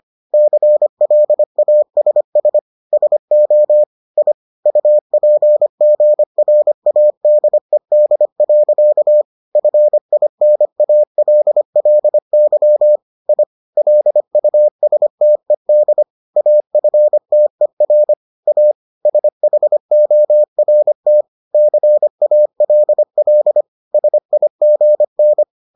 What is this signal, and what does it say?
Morse code, heard as a keyed tone, about 25 wpm, 600 Hz. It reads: CLASS SO I UPGRADED. FINALLY I LUSTED AFTER A SHORT CALL SIGN